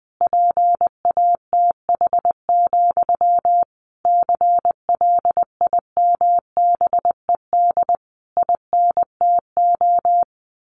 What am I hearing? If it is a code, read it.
Morse code, heard as a keyed tone, about 20 wpm, 700 Hz. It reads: PATH, CLIMBED INTO
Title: A Sound Of Thunder 20wpm